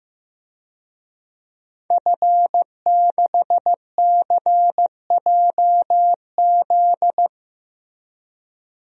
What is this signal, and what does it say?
Morse code, heard as a keyed tone, about 15 wpm, 700 Hz. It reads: F6CJZ
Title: DXCC-100 Call Signs - 5 Characters - Repeated 3x - 15wpm